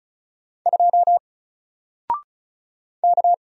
Morse code, heard as a keyed tone, about 35 wpm, 700 Hz. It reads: 2 K
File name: Single-Letter-Number-35wpm